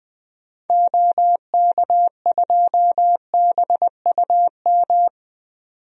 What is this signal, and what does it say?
Morse code, heard as a keyed tone, about 21 wpm, 700 Hz. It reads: OK2BUM
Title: DXCC-100 Call Signs - 6 Characters - 20wpm